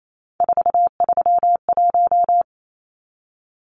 Morse code, heard as a keyed tone, about 28 wpm, 700 Hz. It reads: 431
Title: Numbers - 3-Digit 28wpm